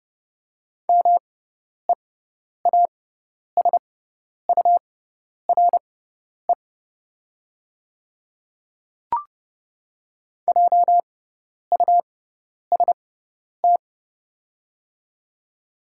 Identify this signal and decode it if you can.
Morse code, heard as a keyed tone, about 30 wpm, spaced as for 10 wpm, 690 Hz. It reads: MEASURE JUST
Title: 6x Character Spacing - Top 500 Words - 30wpm